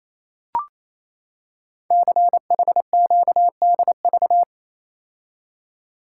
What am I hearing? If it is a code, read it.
Morse code, heard as a keyed tone, about 27 wpm, 700 Hz. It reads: CHQDV
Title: Code Groups - 5 Letter:Numbers - 28wpm